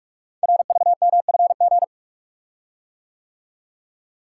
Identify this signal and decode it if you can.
Morse code, heard as a keyed tone, about 44 wpm, 700 Hz. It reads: RVMFG
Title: Code Groups - 5 Letter:Numbers - 45wpm